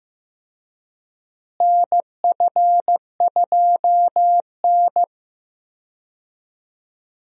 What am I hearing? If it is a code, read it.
Morse code, heard as a keyed tone, about 15 wpm, 700 Hz. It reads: NF2N
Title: USA Call Signs - 4 Characters - Repeated 3x - 15wpm